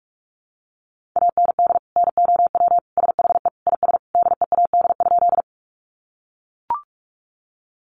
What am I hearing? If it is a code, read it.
Morse code, heard as a keyed tone, about 44 wpm, 700 Hz. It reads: AND NOW SHE IS DEAD?
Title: Sentences from Top 1500 Words 45wpm